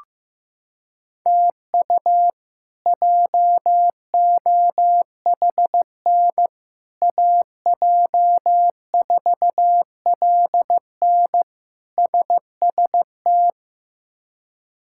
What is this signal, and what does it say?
Morse code, heard as a keyed tone, about 15 wpm, 700 Hz. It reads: TU JOHN AJ4LN SST
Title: K1USN SST QSO Practice - 15wpm